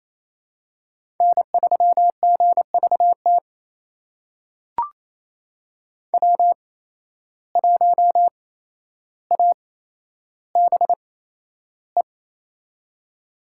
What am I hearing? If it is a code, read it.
Morse code, heard as a keyed tone, about 28 wpm, spaced as for 12 wpm, 700 Hz. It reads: N3GVT W1ABE
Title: USA Call Signs - 5 Characters - 8x Character Spacing - 28wpm